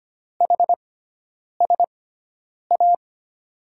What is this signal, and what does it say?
Morse code, heard as a keyed tone, about 25 wpm, spaced as for 9 wpm, 700 Hz. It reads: HSA
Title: USA Call Signs - 5 Characters - 6x Character Spacing - 25wpm